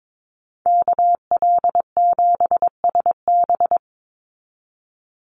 Morse code, heard as a keyed tone, about 22 wpm, 700 Hz. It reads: KL7SB
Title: Call Signs Repeated 2x - 22wpm